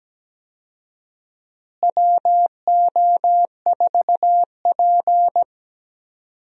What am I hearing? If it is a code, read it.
Morse code, heard as a keyed tone, about 17 wpm, 700 Hz. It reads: WO4P
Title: USA Call Signs - 4 Characters - Repeated 3x - 17wpm